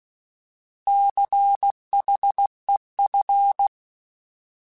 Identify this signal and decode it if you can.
Morse code, heard as a keyed tone, about 16 wpm, 800 Hz. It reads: CHEF
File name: Alphabetical / Alphabetical mp3s / Cs @15